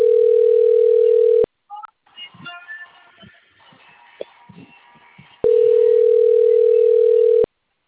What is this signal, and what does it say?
Ringback tone (North America), 440 + 480 Hz.